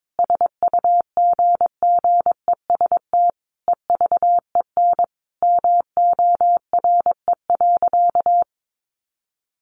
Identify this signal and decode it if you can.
Morse code, heard as a keyed tone, about 22 wpm, 700 Hz. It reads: SUGGEST EVEN MORE.
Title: Sentences from Top 1300 Words 22wpm